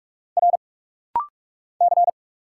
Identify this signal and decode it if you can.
Morse code, heard as a keyed tone, about 46 wpm, 700 Hz. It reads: R C
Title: Single Letters - Rapid-Fire 45wpm